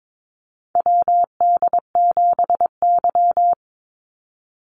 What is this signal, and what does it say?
Morse code, heard as a keyed tone, about 22 wpm, 700 Hz. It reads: WD7Y
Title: NA SOTA Chaser Callsigns 22wpm